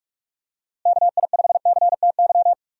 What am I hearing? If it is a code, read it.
Morse code, heard as a keyed tone, about 46 wpm, 700 Hz. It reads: KI5CTY
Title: USA Call Signs - 6 Characters - Repeated 3x - 45wpm